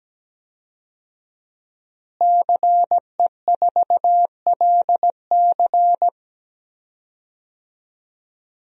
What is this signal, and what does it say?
Morse code, heard as a keyed tone, about 17 wpm, 700 Hz. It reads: CE4LC
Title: DXCC-100 Call Signs - 5 Characters - Repeated 2x - 17wpm